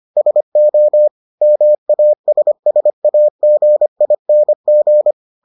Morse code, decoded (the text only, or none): SO MASSAGING